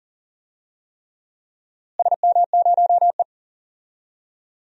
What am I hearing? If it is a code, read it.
Morse code, heard as a keyed tone, about 40 wpm, 700 Hz. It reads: SM0E
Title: DXCC-100 Call Signs - 4 Characters - Repeated 3x - 40wpm